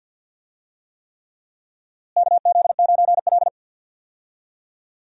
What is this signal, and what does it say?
Morse code, heard as a keyed tone, about 53 wpm, 700 Hz. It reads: KZ9L
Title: USA Call Signs - 4 Characters - Repeated 3x - 50wpm